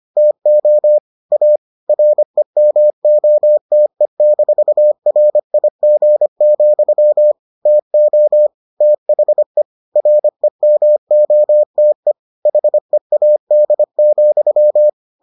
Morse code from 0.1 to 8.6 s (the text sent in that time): TO A REMOTE-RIG, TO